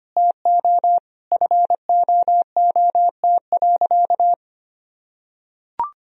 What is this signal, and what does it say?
Morse code, heard as a keyed tone, about 25 wpm, 700 Hz. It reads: TO FOOT.
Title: Sentences from Top 1000 Words - Review Part 3 - 25wpm